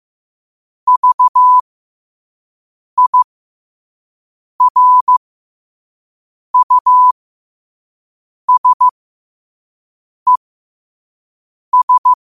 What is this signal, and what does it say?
Morse code, heard as a keyed tone, about 15 wpm, spaced as for 5 wpm, 1000 Hz. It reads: VIRUSES
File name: qotdfiles / 5wpm